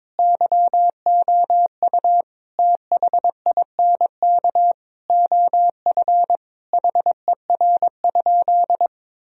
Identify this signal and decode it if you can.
Morse code, heard as a keyed tone, about 22 wpm, 700 Hz. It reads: YOU THINK OF HER?
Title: Sentences from Top 1000 Words - Review Part 4 - 22wpm